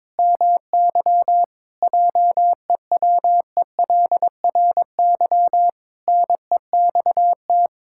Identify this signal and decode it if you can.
Morse code, heard as a keyed tone, about 22 wpm, 700 Hz. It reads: MY JEWELRY NEXT
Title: Pangram Challenge v2 - 22wpm